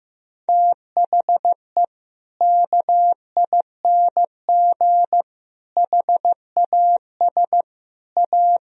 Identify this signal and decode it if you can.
Morse code, heard as a keyed tone, about 15 wpm, 700 Hz. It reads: THE KING HAS A
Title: Sentences from Top 1000 Words - Review Part 4 - 15wpm